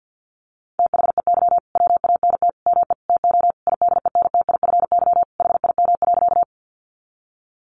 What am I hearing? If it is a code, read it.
Morse code, heard as a keyed tone, about 50 wpm, 700 Hz. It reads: THEY WANT ME TO IDENTIFY HIM.
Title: Sentences from Top 900 Words 50wpm